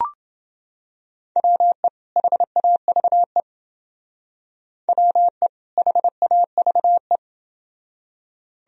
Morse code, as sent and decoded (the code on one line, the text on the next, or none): .-- . / .... .- ...- . / .-- . / .... .- ...- .
WE HAVE WE HAVE